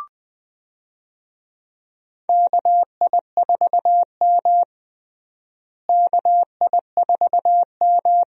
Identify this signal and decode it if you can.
Morse code, heard as a keyed tone, about 20 wpm, 700 Hz. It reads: KI4M KI4M
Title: USA Call Signs - 4 Characters - Repeated 2x - 20wpm